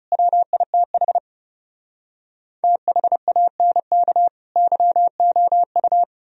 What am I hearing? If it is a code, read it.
Morse code, heard as a keyed tone, about 40 wpm, 700 Hz. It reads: WITH THANK YOU